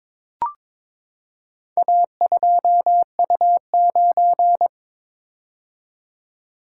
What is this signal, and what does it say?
Morse code, heard as a keyed tone, about 23 wpm, 700 Hz. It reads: A2U9
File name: practice / 22wpm / Four Letter-Number Combinations - 22wpm